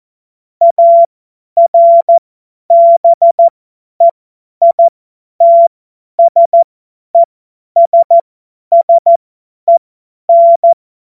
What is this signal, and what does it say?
Morse code, heard as a keyed tone, about 14 wpm, spaced as for 10 wpm, 680 Hz. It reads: ARBEITSESSEN
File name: Morsekurs / Witze / Witz 23. August 2022 50 F70